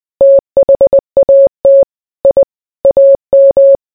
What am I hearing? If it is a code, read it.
Morse code, heard as a keyed tone, about 20 wpm, 550 Hz. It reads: THAT I AM